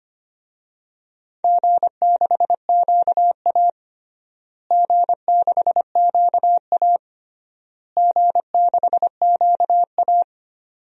Morse code, heard as a keyed tone, about 25 wpm, 700 Hz. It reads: G6QA G6QA G6QA
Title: DXCC-100 Call Signs - 4 Characters - Repeated 3x - 25wpm